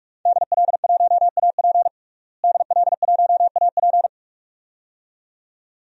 Morse code, heard as a keyed tone, about 46 wpm, 700 Hz. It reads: DL1AP DL1AP
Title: DXCC-100 Call Signs - 5 Characters - Repeated 2x - 45wpm